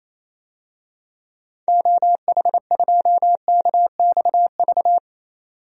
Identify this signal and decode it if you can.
Morse code, heard as a keyed tone, about 28 wpm, 700 Hz. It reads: OH2KXV
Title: DXCC-100 Call Signs - 6 Characters - 28wpm